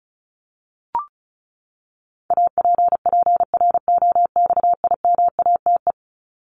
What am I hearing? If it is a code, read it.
Morse code, heard as a keyed tone, about 34 wpm, 700 Hz. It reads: APPROXIMATE
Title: LY Suffix - Top 100 Words with Base Words 35wpm